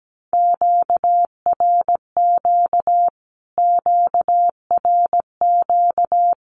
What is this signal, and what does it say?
Morse code, heard as a keyed tone, about 17 wpm, 700 Hz. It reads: QRQ QRQ
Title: Q-Codes Repeated 3x 17wpm